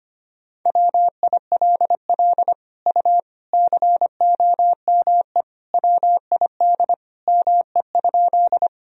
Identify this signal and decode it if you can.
Morse code, heard as a keyed tone, about 25 wpm, 700 Hz. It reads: WILL U COME WID ME?
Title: Sentences from Top 200 Words and Abbreviations 25wpm